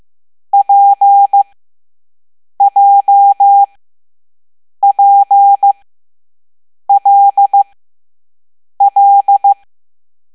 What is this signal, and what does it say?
Morse code, heard as a keyed tone, about 15 wpm, spaced as for 6 wpm, 800 Hz. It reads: PJPLL
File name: Morse / Level7B